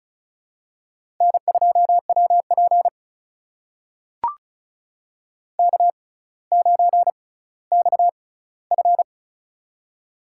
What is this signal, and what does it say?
Morse code, heard as a keyed tone, about 34 wpm, spaced as for 18 wpm, 700 Hz. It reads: N2WP K9XF
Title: USA Call Signs - 4 Characters - 6x Character Spacing - 35wpm